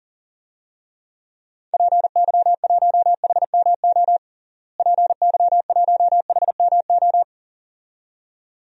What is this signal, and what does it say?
Morse code, heard as a keyed tone, about 37 wpm, 700 Hz. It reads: PY1HMO PY1HMO
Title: DXCC-100 Call Signs - 6 Characters - Repeated 2x - 40wpm